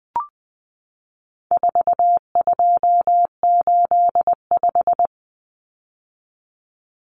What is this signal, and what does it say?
Morse code, heard as a keyed tone, about 20 wpm, 700 Hz. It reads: 4285
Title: Common Cut Numbers - 4-Digit 20wpm